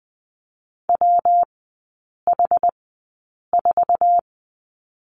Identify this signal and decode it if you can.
Morse code, heard as a keyed tone, about 20 wpm, spaced as for 8 wpm, 690 Hz. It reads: WH4
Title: 133 Lesson 22e - Question Mark - Three Characters 20wpm